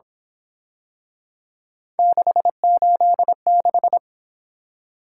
Morse code, heard as a keyed tone, about 28 wpm, 700 Hz. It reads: 686
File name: Speed-Racing - Three Letter-Number Combinations - 22wpm